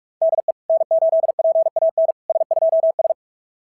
Morse code, heard as a keyed tone, about 44 wpm, 660 Hz. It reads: DE N8PAN S2S
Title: SOTA QSO Practice 45wpm